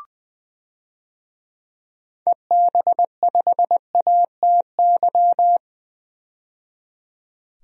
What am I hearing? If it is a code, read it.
Morse code, heard as a keyed tone, about 20 wpm, 700 Hz. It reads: EB5ATY
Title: DXCC-100 Call Signs - 6 Characters - 20wpm